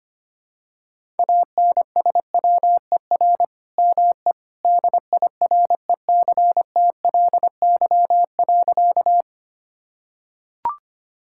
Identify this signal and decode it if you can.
Morse code, heard as a keyed tone, about 25 wpm, 700 Hz. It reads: ANSWER ME DIRECTLY.
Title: Sentences from Top 1000 Words - Review Part 5 - 25wpm